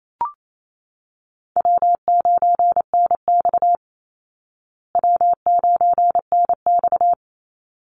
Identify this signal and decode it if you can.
Morse code, heard as a keyed tone, about 28 wpm, 700 Hz. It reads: W9NX W9NX
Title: Easy Call Signs Repeated 2x - 28wpm